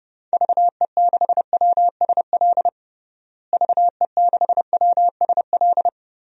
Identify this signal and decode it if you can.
Morse code, heard as a keyed tone, about 30 wpm, 700 Hz. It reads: VE6WSL VE6WSL
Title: DXCC-100 Call Signs - 6 Characters - Repeated 2x - 30wpm